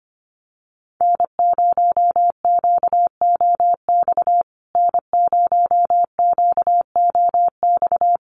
Morse code, heard as a keyed tone, about 25 wpm, 700 Hz. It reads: N0QOX N0QOX
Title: USA Call Signs Repeated 3x 25wpm